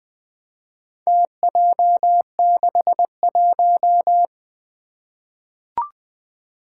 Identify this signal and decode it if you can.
Morse code, heard as a keyed tone, about 21 wpm, 700 Hz. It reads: TJ61
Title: Four Challenging Letter-Number Combinations 20wpm